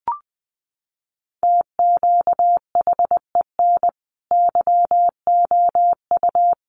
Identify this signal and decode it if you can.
Morse code, heard as a keyed tone, about 19 wpm, 700 Hz. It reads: TQHEN YOU
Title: Sentences from Top 100 Words 20wpm